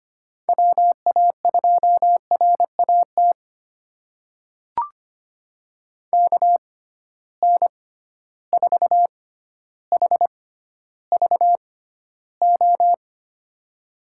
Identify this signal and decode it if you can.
Morse code, heard as a keyed tone, about 25 wpm, spaced as for 13 wpm, 700 Hz. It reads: WA2RAT KN4HVO
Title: USA Call Signs - 6 Characters - 6x Character Spacing - 25wpm